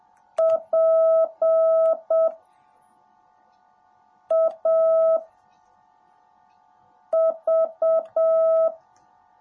Morse code, heard as a keyed tone, about 7 wpm, 650 Hz. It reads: P A V
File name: jh8cbh / 2021cw / cw18a